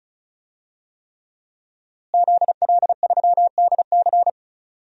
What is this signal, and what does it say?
Morse code, heard as a keyed tone, about 34 wpm, 700 Hz. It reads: ZL3DC
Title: DXCC-100 Call Signs - 5 Characters - Repeated 3x - 35wpm